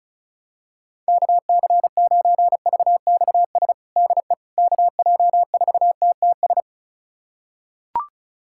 Morse code, heard as a keyed tone, about 36 wpm, 700 Hz. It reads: KC9VXS DE KJ4TTS
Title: USA Callsign DE Callsign - 6 Characters - 35wpm